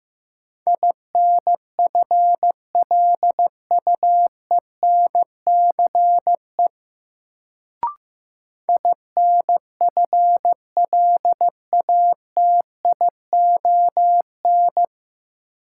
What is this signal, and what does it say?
Morse code, heard as a keyed tone, about 15 wpm, 700 Hz. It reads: INFLUENCE INFLATION
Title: Sentences from Top 1100 Words - New Word Drill Alt - 15wpm